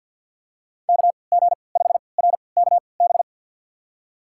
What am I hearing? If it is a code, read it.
Morse code, heard as a keyed tone, about 52 wpm, spaced as for 29 wpm, 700 Hz. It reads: KG5RKB
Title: USA Call Signs - 6 Characters - 3x Character Spacing - 50wpm